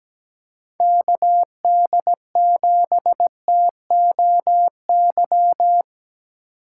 Morse code, heard as a keyed tone, about 17 wpm, 700 Hz. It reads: KD7TOY
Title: USA Call Signs - 4x Character Spacing - 17wpm